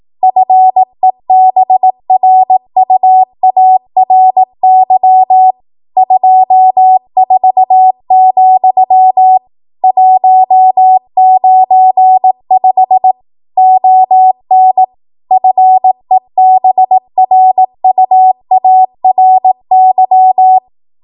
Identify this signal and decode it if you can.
Morse code, heard as a keyed tone, about 18 wpm, 750 Hz. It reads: FEBRUARY 24, 195 ON FEBRUARY